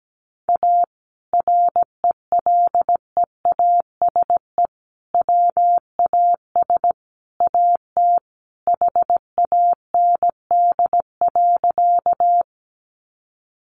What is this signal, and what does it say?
Morse code, heard as a keyed tone, about 17 wpm, 700 Hz. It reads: A RELEASE WAS AT HAND.
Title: Sentences from Top 1100 Words 17wpm